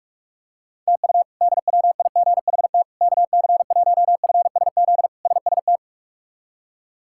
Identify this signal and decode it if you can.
Morse code, heard as a keyed tone, about 46 wpm, 700 Hz. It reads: TU DWIGHT KC1FSZ SST
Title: K1USN SST QSO Practice - 45wpm